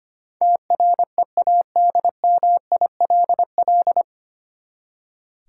Morse code, heard as a keyed tone, about 25 wpm, 700 Hz. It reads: TREADMILL
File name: practice / 25wpm / Easy Words with Hints - 25wpm